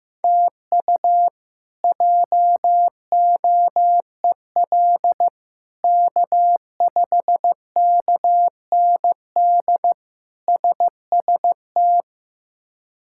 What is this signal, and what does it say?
Morse code, heard as a keyed tone, about 15 wpm, 700 Hz. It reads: TU JOEL K5KND SST
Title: K1USN SST QSO Practice - 15wpm